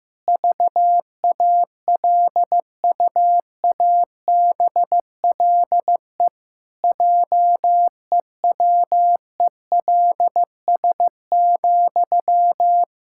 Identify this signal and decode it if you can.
Morse code, heard as a keyed tone, about 15 wpm, 700 Hz. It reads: VALUABLE JEWELS,
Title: Pangram Challenge v2 - 15wpm